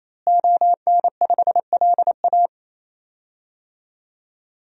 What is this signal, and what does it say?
Morse code, heard as a keyed tone, about 28 wpm, 700 Hz. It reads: ON5LA
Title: DXCC-100 Call Signs - 5 Characters - Repeated 3x - 28wpm